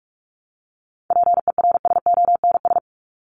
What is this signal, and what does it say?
Morse code, heard as a keyed tone, about 44 wpm, 700 Hz. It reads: PERSONS
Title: ES:S Suffix - Top 100 Words with Base Words 45wpm